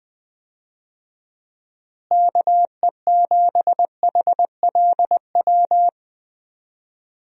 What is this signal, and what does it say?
Morse code, heard as a keyed tone, about 21 wpm, 700 Hz. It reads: KE7HLW
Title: USA Call Signs - 6 Characters - Repeated 2x - 20wpm